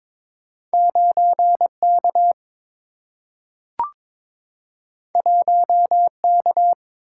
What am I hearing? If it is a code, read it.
Morse code, heard as a keyed tone, about 22 wpm, 700 Hz. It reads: 9K 1K
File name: practice / 22wpm / Two Letter-Number Combinations - 22wpm